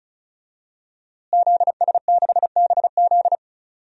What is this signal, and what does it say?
Morse code, heard as a keyed tone, about 36 wpm, 700 Hz. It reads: ZS6BZ
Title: DXCC-100 Call Signs - 5 Characters - Repeated 2x - 35wpm